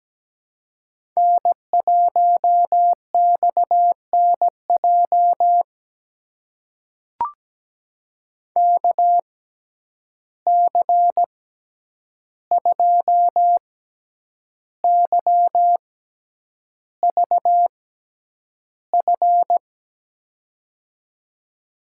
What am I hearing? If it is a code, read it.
Morse code, heard as a keyed tone, about 17 wpm, spaced as for 8 wpm, 700 Hz. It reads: N1XNJ KC2YVF
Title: USA Call Signs - 6x Character Spacing - 17wpm